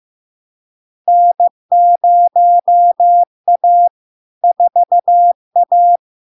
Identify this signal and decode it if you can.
Morse code, heard as a keyed tone, about 15 wpm, 700 Hz. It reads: N0A 4A
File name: LARK-FD-2022-Log-in-CW-15wpm-700-Hz